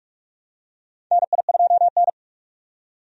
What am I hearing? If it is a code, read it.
Morse code, heard as a keyed tone, about 46 wpm, 700 Hz. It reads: NI2N
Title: USA Call Signs - 4 Characters - 45wpm